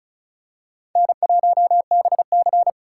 Morse code, heard as a keyed tone, about 36 wpm, 700 Hz. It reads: N1BC
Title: USA Call Signs - 4 Characters - 4x Character Spacing - 35wpm